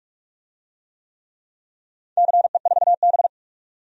Morse code, heard as a keyed tone, about 46 wpm, 700 Hz. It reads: CE4B
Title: DXCC-100 Call Signs - 4 Characters - 45wpm